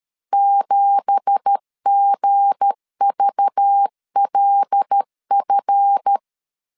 Morse code, heard as a keyed tone, about 13 wpm, 800 Hz. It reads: 7GVLF